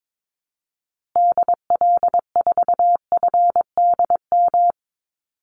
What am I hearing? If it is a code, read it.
Morse code, heard as a keyed tone, about 22 wpm, 700 Hz. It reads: DL4FDM
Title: Call Signs Repeated 2x - 22wpm